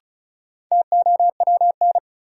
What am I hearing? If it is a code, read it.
Morse code, heard as a keyed tone, about 36 wpm, 700 Hz. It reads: TOWN
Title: Sentences from Top 500 Words - New Word Drill - 35wpm